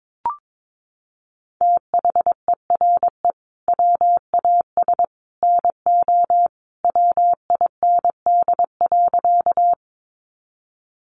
Morse code, heard as a keyed tone, about 22 wpm, 700 Hz. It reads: THERE WAS NO WIND.
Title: Sentences from Top 400 Words 22wpm